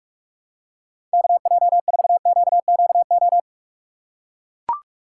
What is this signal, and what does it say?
Morse code, heard as a keyed tone, about 42 wpm, 700 Hz. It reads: KJ4QQO
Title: USA Call Signs - 6 Characters - 4x Character Spacing - 45wpm